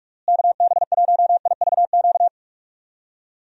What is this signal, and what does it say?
Morse code, heard as a keyed tone, about 48 wpm, 700 Hz. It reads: KB1IVQ
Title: USA Call Signs - 6 Characters - Repeated 3x - 45wpm